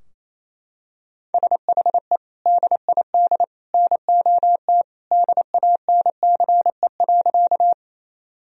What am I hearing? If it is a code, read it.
Morse code, heard as a keyed tone, about 28 wpm, 700 Hz. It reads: SHE DID NOT DANCE.
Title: Sentences from Top 1000 Words - Review Part 4 - 28wpm